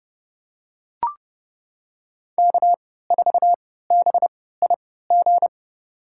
Morse code, heard as a keyed tone, about 30 wpm, spaced as for 17 wpm, 700 Hz. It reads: K4BIG
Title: USA Call Signs - 5 Characters - 3x Character Spacing - 30wpm